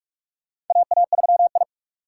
Morse code, heard as a keyed tone, about 42 wpm, 700 Hz. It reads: AA3I
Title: USA Call Signs - 4 Characters - 8x Character Spacing - 45wpm